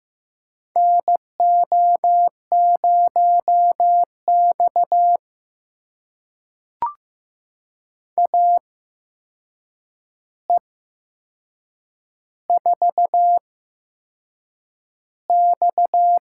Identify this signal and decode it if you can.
Morse code, heard as a keyed tone, about 15 wpm, spaced as for 6 wpm, 700 Hz. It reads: NO0X AE4X
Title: USA Call Signs - 4 Characters - 8x Character Spacing - 15wpm